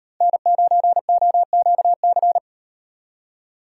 Morse code, heard as a keyed tone, about 38 wpm, 700 Hz. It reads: N9OQC